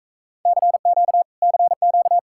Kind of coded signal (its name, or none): Morse code